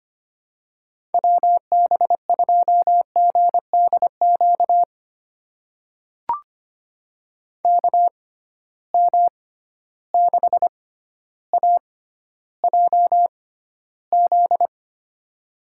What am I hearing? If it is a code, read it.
Morse code, heard as a keyed tone, about 25 wpm, spaced as for 13 wpm, 700 Hz. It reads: WB2GDQ KM6AJZ
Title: USA Call Signs - 6 Characters - 6x Character Spacing - 25wpm